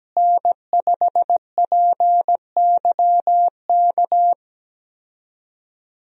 Morse code, heard as a keyed tone, about 17 wpm, 700 Hz. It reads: N5PYK
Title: USA Call Signs - 5 Characters - Repeated 2x - 17wpm